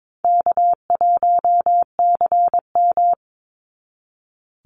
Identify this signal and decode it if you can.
Morse code, heard as a keyed tone, about 22 wpm, 700 Hz. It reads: K1CM
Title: NA SOTA Chaser Callsigns 22wpm